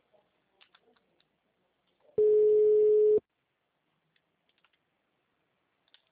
Ringback tone (Europe, 425 Hz).
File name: out- -2036- - - .32677